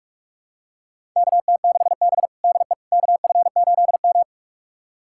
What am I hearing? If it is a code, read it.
Morse code, heard as a keyed tone, about 47 wpm, 700 Hz. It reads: KT6B DE KF8M
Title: USA Callsign DE Callsign - 4 Characters - 45wpm